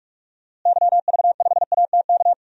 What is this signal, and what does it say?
Morse code, heard as a keyed tone, about 41 wpm, 700 Hz. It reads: YV5ATK
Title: DXCC-100 Call Signs - 6 Characters - Repeated 3x - 45wpm